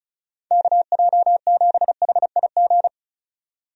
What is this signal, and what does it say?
Morse code, heard as a keyed tone, about 34 wpm, 700 Hz. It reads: KJ7HIG
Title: USA Call Signs - 6 Characters - 35wpm